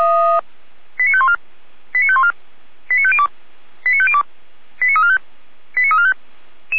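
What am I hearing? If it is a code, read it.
Five-tone selective call (ZVEI), heard as quick runs of five tones, tones of about 70 ms. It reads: 89514 89514 89502 89502 89345 89345
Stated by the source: ZVEI five-tone selective call received with an SDR.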